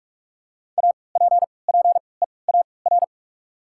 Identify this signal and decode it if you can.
Morse code, heard as a keyed tone, about 43 wpm, 700 Hz. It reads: A P P E A R
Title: 3x Character Spacing - Top 500 Words - 45wpm